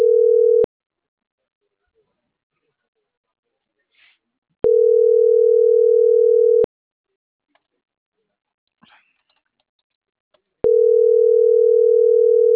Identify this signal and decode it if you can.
Ringback tone (North America), 440 + 480 Hz.